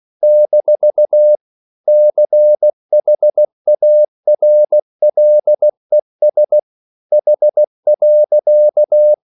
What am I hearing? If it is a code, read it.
Morse code, heard as a keyed tone, about 16 wpm, 600 Hz. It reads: - CHARLES H.
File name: CW / Practice / EnglishQuotations0416